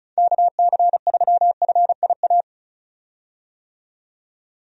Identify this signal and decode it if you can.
Morse code, heard as a keyed tone, about 36 wpm, 700 Hz. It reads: KC3FIA
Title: USA Call Signs - 6 Characters - 35wpm